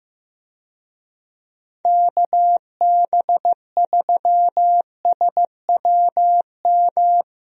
Morse code, heard as a keyed tone, about 15 wpm, 700 Hz. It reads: KB3SWM
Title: USA Call Signs - 6 Characters - Repeated 3x - 15wpm